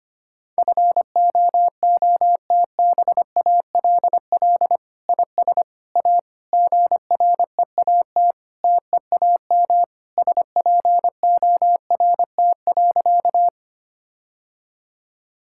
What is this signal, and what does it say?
Morse code, heard as a keyed tone, about 25 wpm, 700 Hz. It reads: FOOTBALL IS A GREAT TEAM SPORT.
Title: Sentences from Top 1600 Words 25wpm